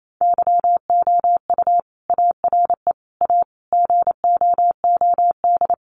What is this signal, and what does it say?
Morse code, heard as a keyed tone, about 28 wpm, 700 Hz. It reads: YOU ARE A GOOD
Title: Sentences from Top 400 Words 28wpm